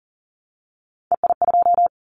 Morse code, heard as a keyed tone, about 38 wpm, 700 Hz. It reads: EI2
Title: Callsign Prefixes for Top 50 DXCC - 40wpm